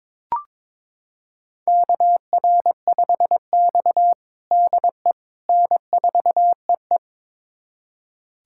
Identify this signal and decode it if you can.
Morse code, heard as a keyed tone, about 22 wpm, 700 Hz. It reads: KR5X DE N4EE
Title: USA Callsign DE Callsign - 4 Characters - 22wpm